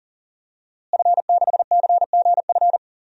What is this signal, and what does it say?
Morse code, heard as a keyed tone, about 37 wpm, 700 Hz. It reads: F6CGF
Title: DXCC-100 Call Signs - 5 Characters - Repeated 2x - 40wpm